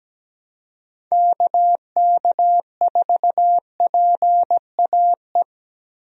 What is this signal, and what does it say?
Morse code, heard as a keyed tone, about 17 wpm, 700 Hz. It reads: KK4PAE
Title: USA Call Signs - 6 Characters - Repeated 3x - 17wpm